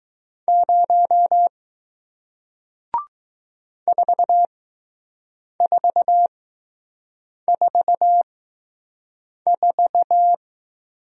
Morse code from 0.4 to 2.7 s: 0

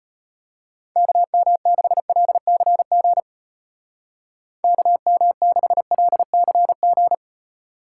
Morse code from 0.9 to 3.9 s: KM6LCG

